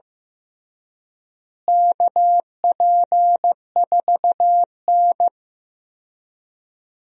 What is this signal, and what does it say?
Morse code, heard as a keyed tone, about 15 wpm, 700 Hz. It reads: KP4N
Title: DXCC-100 Call Signs - 4 Characters - Repeated 3x - 15wpm